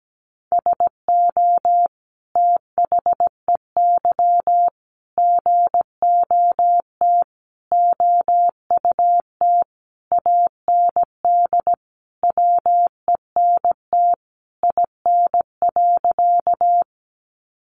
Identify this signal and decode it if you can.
Morse code, heard as a keyed tone, about 17 wpm, 700 Hz. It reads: SO THEY GOT OUT AND WENT IN.
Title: Sentences from Top 300 Words 17wpm